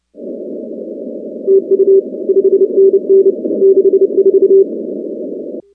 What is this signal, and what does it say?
Morse code, heard as a keyed tone, about 30 wpm, 390 Hz. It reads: TU 5NN 64
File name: contests / 2019-cw-wpx-kt5j-k5pi- -2330 0-19